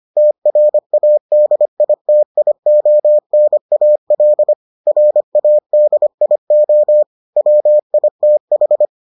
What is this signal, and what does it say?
Morse code, heard as a keyed tone, about 25 wpm, 600 Hz. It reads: TRADITIONAL RADIO WITH